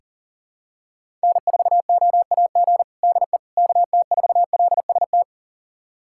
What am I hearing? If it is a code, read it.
Morse code, heard as a keyed tone, about 40 wpm, 700 Hz. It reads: N4OAG DE KT4LST
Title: USA Callsign DE Callsign 40wpm